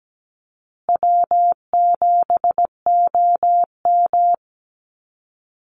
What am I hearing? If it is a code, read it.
Morse code, heard as a keyed tone, about 17 wpm, 700 Hz. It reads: W7OM
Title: Easy Call Signs Repeated 2x - 17wpm